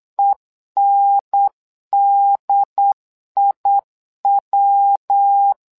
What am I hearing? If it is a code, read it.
Morse code, heard as a keyed tone, about 8 wpm, 800 Hz. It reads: ENDIW
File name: LEZIONI / 3 MWRO / 3 MWRO 8wpm 2606